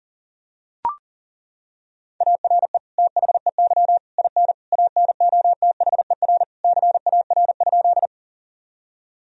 Morse code, heard as a keyed tone, about 40 wpm, 700 Hz. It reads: ARE THEY IN ANOTHER CAR?
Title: Sentences from Top 1000 Words - Review Part 1 - 40wpm